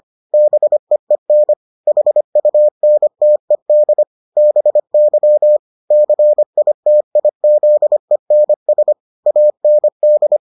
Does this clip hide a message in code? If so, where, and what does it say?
Yes: Morse code from 0.3 to 10.4 s, heard as a keyed tone, about 25 wpm, 600 Hz. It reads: BEEN HUNTED BY CITIZENS AND